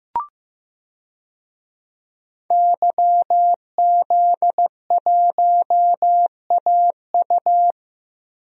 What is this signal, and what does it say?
Morse code, heard as a keyed tone, about 16 wpm, 700 Hz. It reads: YZ1AU
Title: DXCC-100 Call Signs - 5 Characters - Repeated 2x - 15wpm